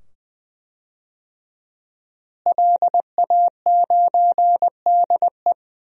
Morse code, heard as a keyed tone, about 20 wpm, 700 Hz. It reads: LA9DE
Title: DXCC-100 Call Signs - 5 Characters - Repeated 2x - 20wpm